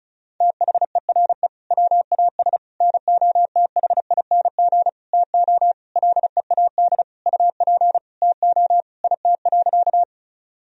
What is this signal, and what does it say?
Morse code, heard as a keyed tone, about 34 wpm, 700 Hz. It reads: THERE WAS NOTHING TO LEAD UP TO IT.
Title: Sentences from Top 1000 Words - Review Part 2 - 35wpm